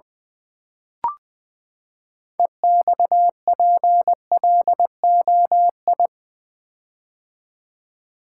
Morse code, heard as a keyed tone, about 20 wpm, 700 Hz. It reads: EXPLOI
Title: Build A Word - 6 Letters At A Time - 12-Letter Words - 20wpm